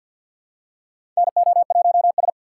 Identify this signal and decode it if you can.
Morse code, heard as a keyed tone, about 48 wpm, 700 Hz. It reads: NO1S
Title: USA Call Signs - 6x Character Spacing - 50wpm